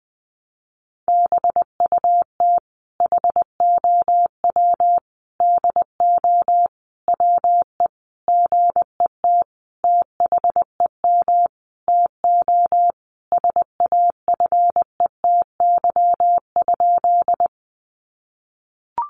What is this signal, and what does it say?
Morse code, heard as a keyed tone, about 20 wpm, 700 Hz. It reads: BUT HOW DO WE GET THEM TO SAFETY?
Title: Sentences from Top 700 Words 20wpm